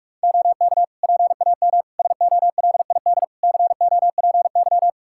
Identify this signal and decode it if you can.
Morse code, heard as a keyed tone, about 42 wpm, 700 Hz. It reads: OK PAM SOLID COPY
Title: Practice QSO - Exchange 3 - More Info - 45wpm